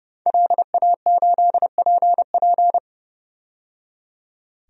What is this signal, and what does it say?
Morse code, heard as a keyed tone, about 30 wpm, 700 Hz. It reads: LA8PP
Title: DXCC-100 Call Signs - 5 Characters - 30wpm